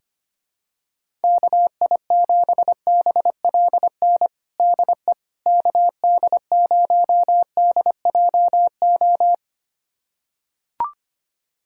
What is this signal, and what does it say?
Morse code, heard as a keyed tone, about 25 wpm, 700 Hz. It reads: KI7BLN DE KD0DJO
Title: USA Callsign DE Callsign 25wpm